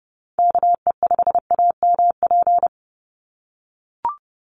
Morse code, heard as a keyed tone, about 29 wpm, 700 Hz. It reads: KE5AMP
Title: USA Call Signs 30wpm